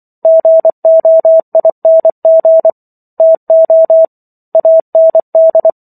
Morse code, heard as a keyed tone, about 24 wpm, 650 Hz. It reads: GOING TO AND